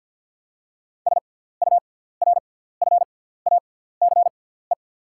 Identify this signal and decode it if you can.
Morse code, heard as a keyed tone, about 50 wpm, spaced as for 17 wpm, 690 Hz. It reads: SURFACE
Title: 6x Character Spacing - Top 500 Words - 50wpm